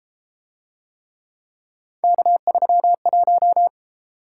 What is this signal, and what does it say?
Morse code, heard as a keyed tone, about 31 wpm, 700 Hz. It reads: K31